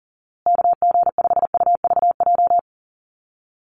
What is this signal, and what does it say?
Morse code, heard as a keyed tone, about 40 wpm, 700 Hz. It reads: KG5UVJ
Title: USA Call Signs Repeated 3x 40wpm